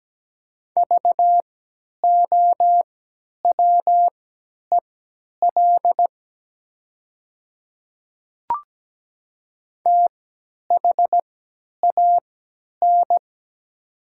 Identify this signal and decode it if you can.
Morse code, heard as a keyed tone, about 17 wpm, spaced as for 10 wpm, 700 Hz. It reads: VOWEL THAN
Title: 3x Character Spacing - Top 500 Words - 17wpm